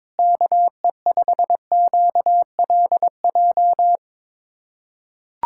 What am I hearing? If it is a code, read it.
Morse code, heard as a keyed tone, about 21 wpm, 700 Hz. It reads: KE5QLJ
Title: USA Callsign DE Callsign - 6 Characters - 22wpm